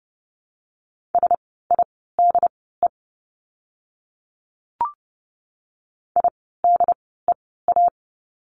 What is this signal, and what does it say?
Morse code, heard as a keyed tone, about 30 wpm, spaced as for 17 wpm, 700 Hz. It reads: SIDE IDEA
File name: course / 30:10wpm / 108 Lesson 18f - Letter m - Words 30wpm